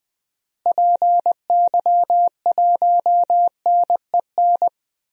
Morse code, heard as a keyed tone, about 20 wpm, 700 Hz. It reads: PY1NEN
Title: DXCC-100 Call Signs - 6 Characters - Repeated 3x - 20wpm